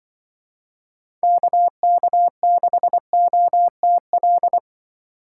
Morse code, heard as a keyed tone, about 24 wpm, 700 Hz. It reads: KK6OTL
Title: Speed Racing - USA 2x3 Call Signs 20wpm